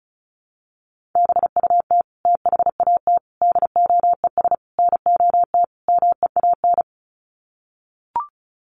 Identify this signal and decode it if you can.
Morse code, heard as a keyed tone, about 34 wpm, 700 Hz. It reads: BUT THAT DOES NOT MEAN
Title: Sets of 5 Words 35wpm